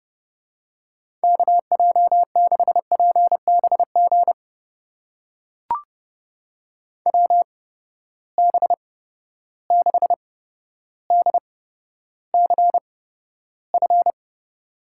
Morse code, heard as a keyed tone, about 30 wpm, spaced as for 13 wpm, 700 Hz. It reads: KJ6PBG WB6DCF
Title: USA Call Signs - 8x Character Spacing - 30wpm